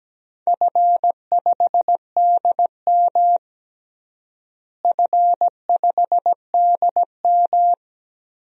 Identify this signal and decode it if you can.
Morse code, heard as a keyed tone, about 17 wpm, 700 Hz. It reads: F5DM F5DM
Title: DXCC-100 Call Signs - 4 Characters - Repeated 3x - 17wpm